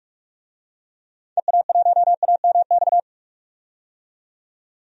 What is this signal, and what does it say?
Morse code, heard as a keyed tone, about 45 wpm, 700 Hz. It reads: EA1AMX
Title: DXCC-100 Call Signs - 6 Characters - 45wpm